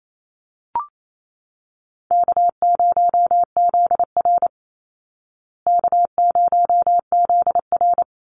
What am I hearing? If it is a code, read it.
Morse code, heard as a keyed tone, about 28 wpm, 700 Hz. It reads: K0ZR K0ZR
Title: Call Signs Repeated 3x - 28wpm